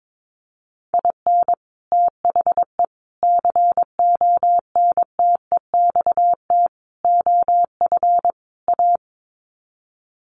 Morse code, heard as a keyed tone, about 22 wpm, 700 Hz. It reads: IN THE CONTEXT OF A
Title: Sets of 5 Words Encore 22wpm